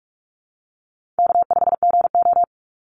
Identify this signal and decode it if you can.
Morse code, heard as a keyed tone, about 46 wpm, 700 Hz. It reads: K5GO
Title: Easy Call Signs - 45wpm